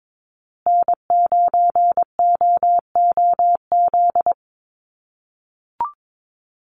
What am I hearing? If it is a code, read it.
Morse code, heard as a keyed tone, about 22 wpm, 700 Hz. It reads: N9OOZ
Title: USA Call Signs Repeated 3x 22wpm